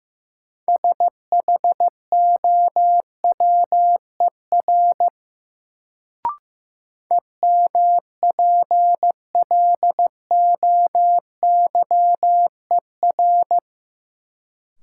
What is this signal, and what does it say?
Morse code, heard as a keyed tone, about 15 wpm, 700 Hz. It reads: SHOWER EMPLOYER
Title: Sentences from Top 1600 Words - New Word Drill Alt - 15wpm